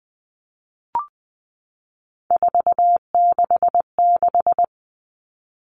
Morse code, heard as a keyed tone, about 20 wpm, 700 Hz. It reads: 466
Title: All Cut Numbers - 3-Digit 20wpm